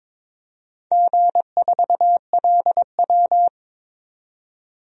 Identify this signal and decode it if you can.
Morse code, heard as a keyed tone, about 22 wpm, 700 Hz. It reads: G4LW
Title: DXCC-100 Call Signs - 4 Characters - 22wpm